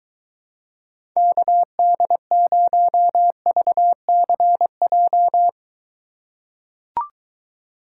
Morse code, heard as a keyed tone, about 23 wpm, 700 Hz. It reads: KD0VCJ